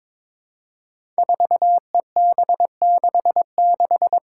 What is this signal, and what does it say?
Morse code, heard as a keyed tone, about 22 wpm, 700 Hz. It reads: 4EB66
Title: Five Challenging Letter-Number Combinations 22wpm